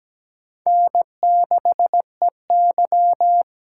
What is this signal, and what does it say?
Morse code, heard as a keyed tone, about 17 wpm, 700 Hz. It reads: N6EY
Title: USA Call Signs - 4 Characters - 3x Character Spacing - 17wpm